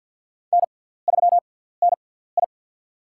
Morse code, heard as a keyed tone, about 48 wpm, spaced as for 17 wpm, 690 Hz. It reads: N3NI
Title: USA Call Signs - 6x Character Spacing - 50wpm